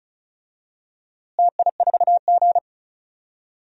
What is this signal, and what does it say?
Morse code, heard as a keyed tone, about 36 wpm, 700 Hz. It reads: TI4G
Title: DXCC-100 Call Signs - 4 Characters - Repeated 2x - 35wpm